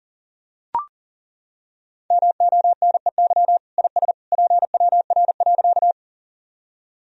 Morse code, heard as a keyed tone, about 40 wpm, 700 Hz. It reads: MONEY IS PWR.
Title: Sentences from Top 500 Words and Abbreviations 40wpm